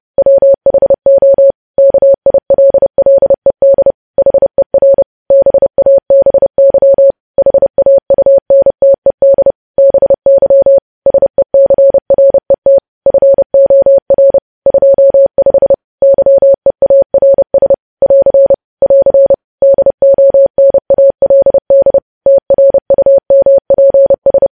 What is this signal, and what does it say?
Morse code, heard as a keyed tone, about 30 wpm, 550 Hz. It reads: WHO KILLED HER BABY HAUNTED BY SECRET FOR 25 YEARS <AR> <AR> DONALD TRUMPS
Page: news headlines in morse code at 30 wpm